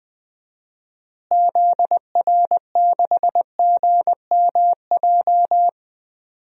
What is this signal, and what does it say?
Morse code, heard as a keyed tone, about 20 wpm, 700 Hz. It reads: ZR6GMJ
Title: DXCC-100 Call Signs - 6 Characters - Repeated 3x - 20wpm